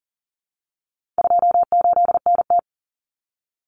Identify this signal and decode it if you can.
Morse code, heard as a keyed tone, about 40 wpm, 700 Hz. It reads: 28NT
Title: Common Cut Numbers - 4-Digit 40wpm